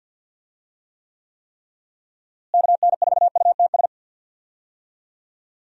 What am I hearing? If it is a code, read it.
Morse code, heard as a keyed tone, about 47 wpm, 700 Hz. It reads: KN4UTS
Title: USA Call Signs - 6 Characters - Repeated 3x - 50wpm